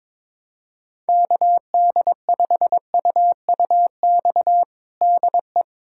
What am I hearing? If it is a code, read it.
Morse code, heard as a keyed tone, about 21 wpm, 700 Hz. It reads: KD5UUX DE
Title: USA Callsign DE Callsign 22wpm